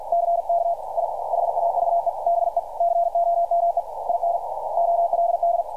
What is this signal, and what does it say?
Morse code, heard as a keyed tone, about 48 wpm, 690 Hz, with radio noise. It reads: CK I HAVE BECOME IN OLD
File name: archived / CRSnet